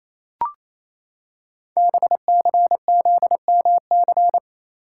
Morse code, heard as a keyed tone, about 29 wpm, 700 Hz. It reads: BCZMC
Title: Code Groups - 5 Letters - 28wpm